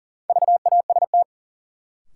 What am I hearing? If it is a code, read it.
Morse code, heard as a keyed tone, about 40 wpm, 700 Hz. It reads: VAST
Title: Sentences from Top 1300 Words - New Word Drill Alt - 40wpm